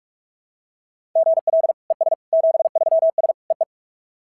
Morse code, heard as a keyed tone, about 41 wpm, 690 Hz. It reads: GL ES 73S EE
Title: SOTA QSO Practice 45wpm